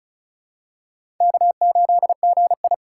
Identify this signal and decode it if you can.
Morse code, heard as a keyed tone, about 36 wpm, 700 Hz. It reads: K8GI
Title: USA Call Signs - 4 Characters - 4x Character Spacing - 35wpm